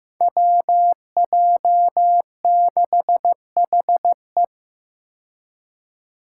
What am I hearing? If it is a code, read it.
Morse code, heard as a keyed tone, about 15 wpm, 700 Hz. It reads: WJ6HE
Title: Five Challenging Letter-Number Combinations 15wpm